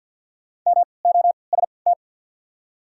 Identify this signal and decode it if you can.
Morse code, heard as a keyed tone, about 54 wpm, spaced as for 29 wpm, 690 Hz. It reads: MOST
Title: 3x Character Spacing - Top 500 Words - 50wpm